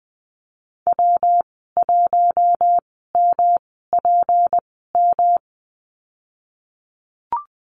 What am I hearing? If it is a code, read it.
Morse code, heard as a keyed tone, about 20 wpm, spaced as for 15 wpm, 700 Hz. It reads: W1MPM
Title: 231 Lesson 37g - Letter x - Call Signs 20wpm